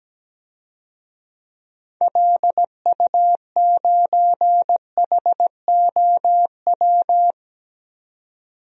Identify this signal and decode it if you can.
Morse code, heard as a keyed tone, about 17 wpm, 700 Hz. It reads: LU9HOW
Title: DXCC-100 Call Signs - 6 Characters - 17wpm